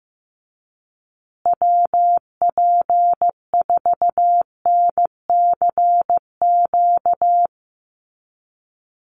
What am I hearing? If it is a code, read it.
Morse code, heard as a keyed tone, about 15 wpm, 700 Hz. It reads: WP4NCQ
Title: USA Call Signs Repeated 2x 15wpm